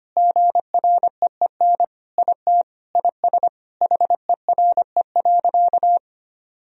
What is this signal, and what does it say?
Morse code, heard as a keyed tone, about 25 wpm, 700 Hz. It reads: GREEN IT IS HERE.
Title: Sentences from Top 1000 Words - Review Part 2 - 25wpm